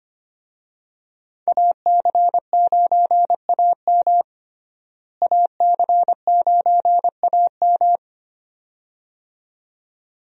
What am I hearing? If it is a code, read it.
Morse code, heard as a keyed tone, about 25 wpm, 700 Hz. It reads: AC9AM AC9AM
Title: USA Call Signs - 5 Characters - Repeated 2x - 25wpm